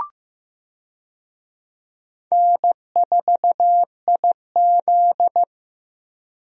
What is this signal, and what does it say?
Morse code, heard as a keyed tone, about 16 wpm, 700 Hz. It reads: N4IZ
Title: USA Call Signs - 4 Characters - Repeated 2x - 15wpm